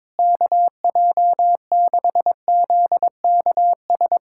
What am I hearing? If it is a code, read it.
Morse code, heard as a keyed tone, about 23 wpm, 700 Hz. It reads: KJ6ZKS
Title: USA Callsign DE Callsign - 6 Characters - 22wpm